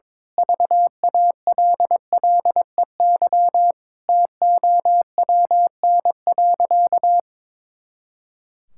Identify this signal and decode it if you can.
Morse code, heard as a keyed tone, about 22 wpm, 700 Hz. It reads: VALLEY TOWN.
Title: Pangram Challenge v2 - 22wpm